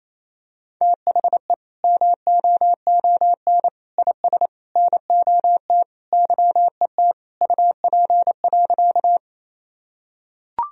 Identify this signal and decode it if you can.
Morse code, heard as a keyed tone, about 28 wpm, 700 Hz. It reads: THE MOON IS NOT YET UP.
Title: Sentences from Top 1000 Words - Review Part 3 - 28wpm